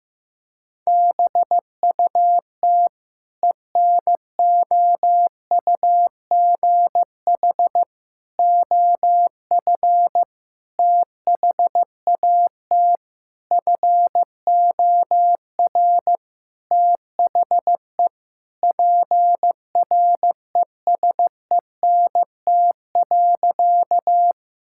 Morse code, heard as a keyed tone, about 15 wpm, 700 Hz. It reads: BUT ENOUGH OF THAT FOR THE PRESENT.
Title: Sentences from Top 1000 Words - Review Part 5 - 15wpm